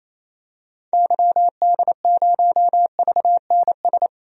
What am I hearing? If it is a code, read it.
Morse code, heard as a keyed tone, about 27 wpm, 700 Hz. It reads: YD0VNS
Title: DXCC-100 Call Signs - 6 Characters - 28wpm